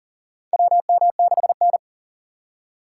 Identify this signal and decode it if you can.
Morse code, heard as a keyed tone, about 40 wpm, 700 Hz. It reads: WM6N
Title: USA Call Signs - 4 Characters - 4x Character Spacing - 40wpm